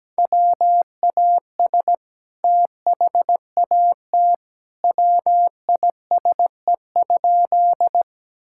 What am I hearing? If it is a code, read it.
Morse code, heard as a keyed tone, about 17 wpm, 700 Hz. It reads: WAS THAT WISE?
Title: Sentences from Top 1600 Words 17wpm